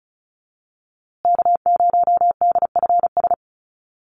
Morse code, heard as a keyed tone, about 34 wpm, 700 Hz. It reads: K0DFS
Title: USA Call Signs Repeated 3x 35wpm